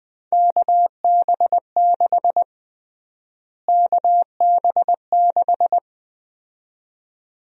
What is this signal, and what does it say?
Morse code, heard as a keyed tone, about 20 wpm, 700 Hz. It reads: KB6 KB6
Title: USA Call Sign Prefixes - Repeated 2x - 20wpm